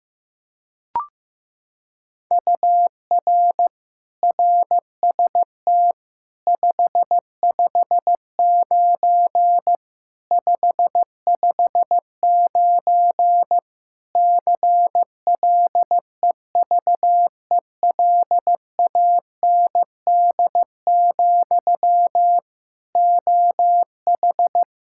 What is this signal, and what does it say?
Morse code, heard as a keyed tone, about 15 wpm, 700 Hz. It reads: UR RST 559 559 CLEVELAND, OH
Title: K1USN SST QSO Practice 15wpm